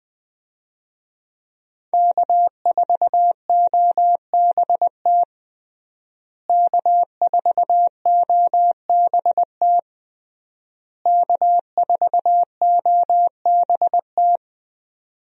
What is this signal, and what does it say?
Morse code, heard as a keyed tone, about 20 wpm, 700 Hz. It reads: K4OBT K4OBT K4OBT
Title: USA Call Signs - 5 Characters - Repeated 3x - 20wpm